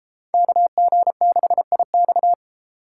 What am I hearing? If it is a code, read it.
Morse code, heard as a keyed tone, about 31 wpm, 700 Hz. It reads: KG6IX